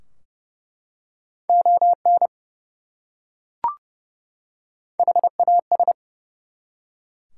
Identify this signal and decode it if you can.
Morse code, heard as a keyed tone, about 30 wpm, 700 Hz. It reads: ON HAS
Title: Top 180 QSO Elements 30wpm